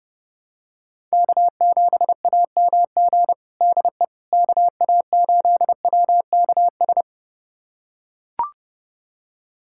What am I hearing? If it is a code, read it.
Morse code, heard as a keyed tone, about 30 wpm, 700 Hz. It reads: K7AMG DE KA8WKS
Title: USA Callsign DE Callsign 30wpm